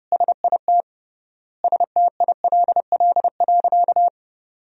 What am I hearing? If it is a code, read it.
Morse code, heard as a keyed tone, about 30 wpm, 700 Hz. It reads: SIT STILL.
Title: 3x Word-Spacing - Sentences from Top 500 Words - 30wpm